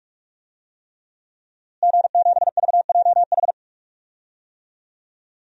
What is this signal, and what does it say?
Morse code, heard as a keyed tone, about 46 wpm, 700 Hz. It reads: G7VJH
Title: DXCC-100 Call Signs - 5 Characters - Repeated 3x - 45wpm